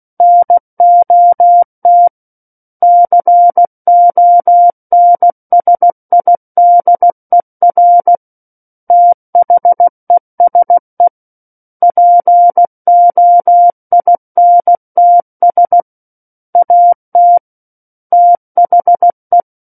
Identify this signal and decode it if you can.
Morse code, heard as a keyed tone, about 16 wpm, 700 Hz. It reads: NOT CONSIDER THESE POINTS AT THE
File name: Book One / 07-Chapter 7 - 16 WPM